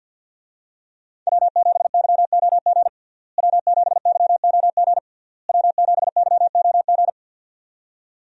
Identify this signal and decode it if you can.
Morse code, heard as a keyed tone, about 49 wpm, 700 Hz. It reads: W7YOG W7YOG W7YOG
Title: USA Call Signs - 5 Characters - Repeated 3x - 50wpm